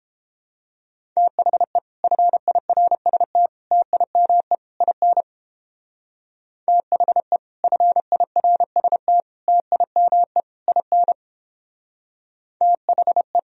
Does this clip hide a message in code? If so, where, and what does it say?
Yes: Morse code from 1.2 to 13.4 s, heard as a keyed tone, about 29 wpm, 700 Hz. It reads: THE FIRST TIME IN THE FIRST TIME IN THE